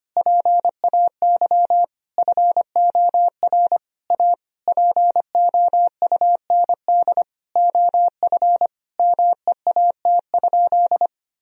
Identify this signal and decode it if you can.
Morse code, heard as a keyed tone, about 25 wpm, 700 Hz. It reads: PAY FOR A POUND OF MEAT?
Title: Sentences from Top 1000 Words - Review Part 3 - 25wpm